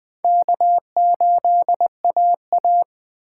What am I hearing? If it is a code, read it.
Morse code, heard as a keyed tone, about 21 wpm, 700 Hz. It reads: K8AA